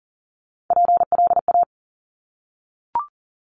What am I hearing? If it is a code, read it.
Morse code, heard as a keyed tone, about 42 wpm, 690 Hz. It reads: PLA
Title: Top Three Letter Combinations in Words 40wpm